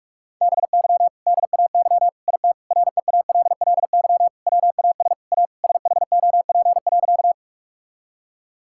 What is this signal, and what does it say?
Morse code, heard as a keyed tone, about 46 wpm, 700 Hz. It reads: BY DAY IT REALLY WAS A SHOP.
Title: Sentences from Top 1600 Words 45wpm